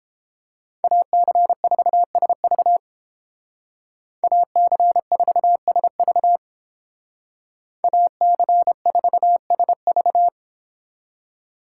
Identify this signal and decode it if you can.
Morse code, heard as a keyed tone, about 29 wpm, 700 Hz. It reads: AC4SV AC4SV AC4SV